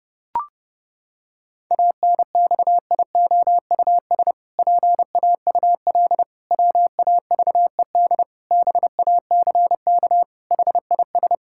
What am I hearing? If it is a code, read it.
Morse code, heard as a keyed tone, about 30 wpm, 700 Hz. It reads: ANXIOUS PAUL WAVED BACK HIS
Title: Pangram Challenge v2 - 30wpm